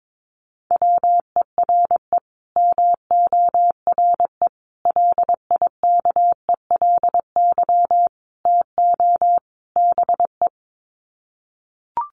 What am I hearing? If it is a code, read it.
Morse code, heard as a keyed tone, about 22 wpm, 700 Hz. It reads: WERE MORE LIKELY TO BE
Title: Sets of 5 Words Encore2 22wpm